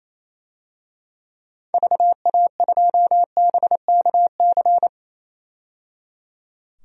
Morse code, heard as a keyed tone, about 28 wpm, 700 Hz. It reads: VA2BKC
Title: DXCC-100 Call Signs - 6 Characters - 28wpm